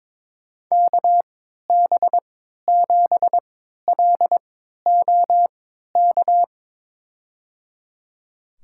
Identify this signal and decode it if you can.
Morse code, heard as a keyed tone, about 22 wpm, spaced as for 13 wpm, 700 Hz. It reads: KB7LOK
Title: USA Call Signs - 6 Characters - 3x Character Spacing - 22wpm